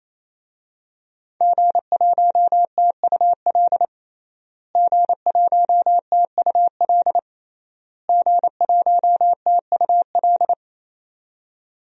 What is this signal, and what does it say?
Morse code, heard as a keyed tone, about 28 wpm, 700 Hz. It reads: G1TUL G1TUL G1TUL
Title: DXCC-100 Call Signs - 5 Characters - Repeated 3x - 28wpm